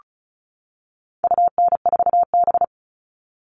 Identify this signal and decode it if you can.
Morse code, heard as a keyed tone, about 34 wpm, 700 Hz. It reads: UN4B
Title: All Cut Numbers - 4-Digit 35wpm